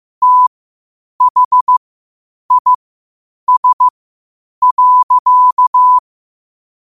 Morse code, heard as a keyed tone, about 15 wpm, spaced as for 8 wpm, 1000 Hz. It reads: THIS.
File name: qotdfiles / 8wpm